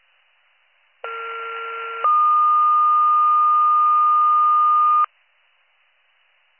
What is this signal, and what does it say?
Two-tone page, heard as a one-second tone, then a three-second tone, about 1370 then 1220 Hz. It is heard from 1.0 to 5.0 s.